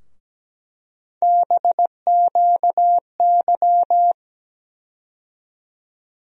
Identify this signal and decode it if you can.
Morse code, heard as a keyed tone, about 17 wpm, 700 Hz. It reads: BQY
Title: USA Call Sign Postfixes - 3 Letters 17wpm